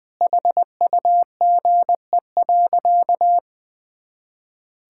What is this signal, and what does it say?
Morse code, heard as a keyed tone, about 20 wpm, 700 Hz. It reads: HUGE.
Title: Sentences with 6-Word Phrases - 20wpm